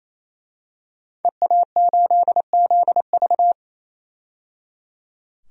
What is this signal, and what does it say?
Morse code, heard as a keyed tone, about 28 wpm, 700 Hz. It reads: EA8ZV
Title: DXCC-100 Call Signs - 5 Characters - 28wpm